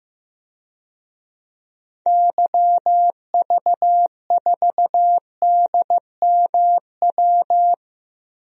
Morse code, heard as a keyed tone, about 15 wpm, 700 Hz. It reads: YV4DMW
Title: DXCC-100 Call Signs - 6 Characters - Repeated 2x - 15wpm